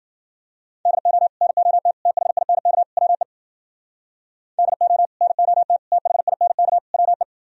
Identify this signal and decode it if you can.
Morse code, heard as a keyed tone, about 51 wpm, 700 Hz. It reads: DO NOT THINK WE DO NOT THINK WE